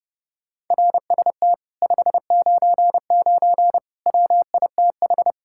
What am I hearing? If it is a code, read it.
Morse code, heard as a keyed tone, about 31 wpm, 700 Hz. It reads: RST 599 WITH